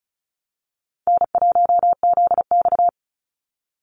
Morse code, heard as a keyed tone, about 34 wpm, 700 Hz. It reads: N1ZX
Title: Easy Call Signs Repeated 3x - 35wpm